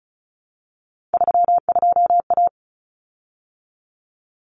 Morse code, heard as a keyed tone, about 34 wpm, 700 Hz. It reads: 32A
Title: Common Cut Numbers - 3-Digit 35wpm